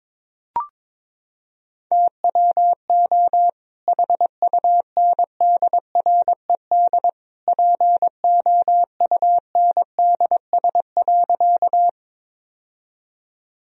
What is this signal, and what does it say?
Morse code, heard as a keyed tone, about 22 wpm, 700 Hz. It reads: TWO HUNDRED POUNDS.
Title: Sentences from Top 1000 Words - Review Part 2 - 22wpm